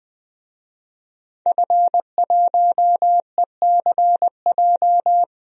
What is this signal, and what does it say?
Morse code, heard as a keyed tone, about 20 wpm, 700 Hz. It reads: F1ECJ
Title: DXCC-100 Call Signs - 5 Characters - 20wpm